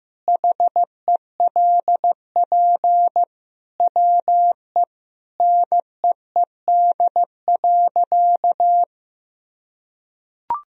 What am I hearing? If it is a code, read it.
Morse code, heard as a keyed tone, about 16 wpm, 700 Hz. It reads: HELP WE NEED.
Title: Sentences from Top 1000 Words - Review Part 1 - 15wpm